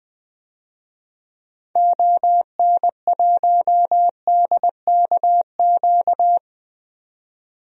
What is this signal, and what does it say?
Morse code, heard as a keyed tone, about 20 wpm, 700 Hz. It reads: ON1DKQ
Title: DXCC-100 Call Signs - 6 Characters - 20wpm